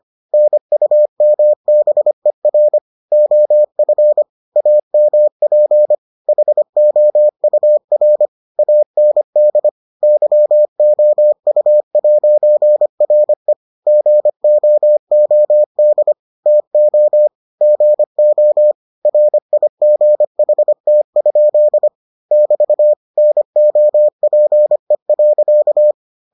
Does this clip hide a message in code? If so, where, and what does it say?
Yes: Morse code from 0.3 to 25.9 s, heard as a keyed tone, about 25 wpm, 600 Hz. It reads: NUMBER OF AMP HOUR AND YOU'RE GOOD TO GO RIGHT? <BT> NOPE.